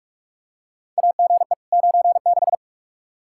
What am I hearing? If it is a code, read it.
Morse code, heard as a keyed tone, about 46 wpm, 700 Hz. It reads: AGE 96
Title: Practice QSO - Exchange 3 - More Info - 45wpm